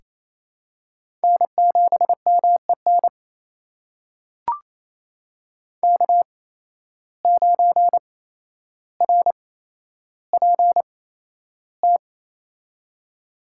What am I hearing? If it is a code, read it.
Morse code, heard as a keyed tone, about 27 wpm, spaced as for 12 wpm, 700 Hz. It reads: N7MEN K9RPT
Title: USA Call Signs - 5 Characters - 8x Character Spacing - 28wpm